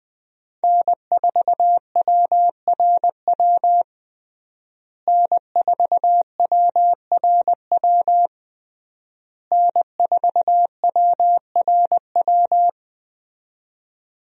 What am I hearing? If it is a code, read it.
Morse code, heard as a keyed tone, about 20 wpm, 700 Hz. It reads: N4WRW N4WRW N4WRW
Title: USA Call Signs - 5 Characters - Repeated 3x - 20wpm